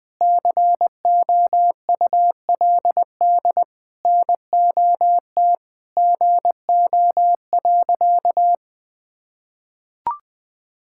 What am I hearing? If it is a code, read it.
Morse code, heard as a keyed tone, about 20 wpm, 700 Hz. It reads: COULD NOT GO.
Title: Sentences with 5-Word Phrases - 20wpm